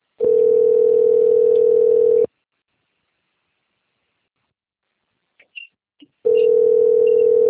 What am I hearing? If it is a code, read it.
Ringback tone (North America), 440 + 480 Hz.
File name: rg-1600-unknown- - - .23263